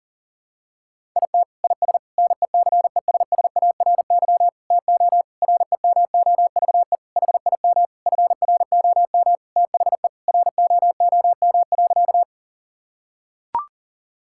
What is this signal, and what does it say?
Morse code, heard as a keyed tone, about 37 wpm, 700 Hz. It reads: IT IS NECESSARY TO REMOVE HIM FROM THE ROOM.
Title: Sentences from Top 1000 Words - Review Part 5 - 40wpm